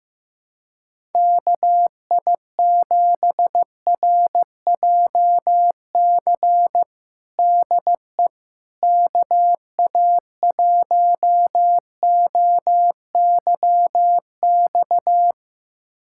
Morse code, heard as a keyed tone, about 15 wpm, 700 Hz. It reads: KI7RJC DE KA1OYX
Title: USA Callsign DE Callsign - 6 Characters - 15wpm